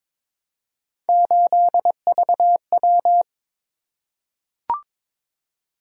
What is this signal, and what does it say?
Morse code, heard as a keyed tone, about 22 wpm, 700 Hz. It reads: 8VW
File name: practice / 22wpm / Three Letter-Number Combinations - 22wpm